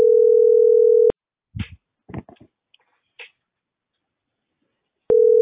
Ringback tone (North America), 440 + 480 Hz.